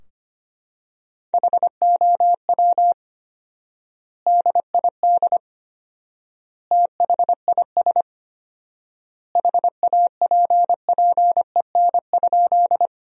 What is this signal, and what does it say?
Morse code, heard as a keyed tone, about 25 wpm, 700 Hz. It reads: HOW DID THIS HAPPEN?
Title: 4x Word-Spacing - Sentences from Top 400 Words - 25wpm